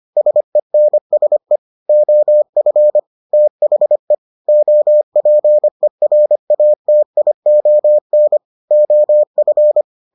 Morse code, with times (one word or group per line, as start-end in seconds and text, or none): 0.2-1.6 SENSE
1.9-3.0 OF
3.3-4.1 THE
4.5-8.4 OPERATION
8.7-9.8 OF